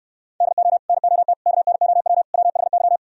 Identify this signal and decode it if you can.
Morse code, heard as a keyed tone, about 69 wpm, 700 Hz. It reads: DO NOT KNOW WHY